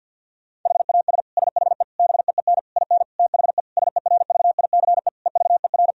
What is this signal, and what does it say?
Morse code, heard as a keyed tone, about 50 wpm, 700 Hz. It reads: HAS SHE BEEN IN THE SERVICE EVER
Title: Sentences from Top 1000 Words - Review Part 4 - 50wpm